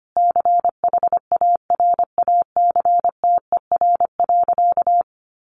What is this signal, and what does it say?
Morse code, heard as a keyed tone, about 25 wpm, 700 Hz. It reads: CHARACTER.
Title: Sentences from Top 1200 Words 25wpm